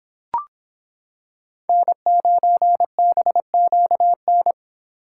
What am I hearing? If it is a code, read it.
Morse code, heard as a keyed tone, about 26 wpm, 700 Hz. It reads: N9BQN